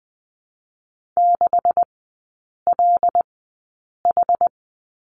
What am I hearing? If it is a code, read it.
Morse code, heard as a keyed tone, about 20 wpm, spaced as for 8 wpm, 700 Hz. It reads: 6LH
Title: 127 Lesson 21e - Number 6 - Three Characters 20wpm